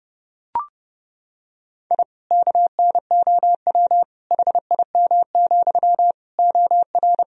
Morse code, heard as a keyed tone, about 30 wpm, 700 Hz. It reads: I KNOW HIM, OR
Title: Sentences from Top 1000 Words - Review Part 4 - 30wpm